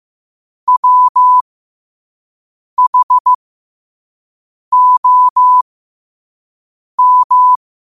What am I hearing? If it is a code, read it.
Morse code, heard as a keyed tone, about 15 wpm, spaced as for 5 wpm, 1000 Hz. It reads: WHOM
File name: qotdfiles / 5wpm